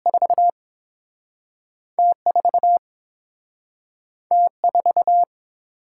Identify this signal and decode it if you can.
Morse code, heard as a keyed tone, about 30 wpm, 700 Hz. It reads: 4 T 4 T 4